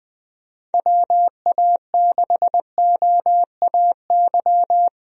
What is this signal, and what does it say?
Morse code, heard as a keyed tone, about 20 wpm, 700 Hz. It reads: WA6OAY
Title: USA Call Signs - 6 Characters - Repeated 2x - 20wpm